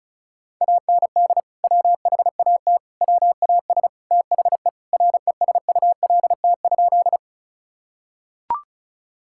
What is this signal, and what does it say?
Morse code, heard as a keyed tone, about 36 wpm, 700 Hz. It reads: AND WHAT WAS THE RESULT?
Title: Sentences from Top 1000 Words - Review Part 5 - 35wpm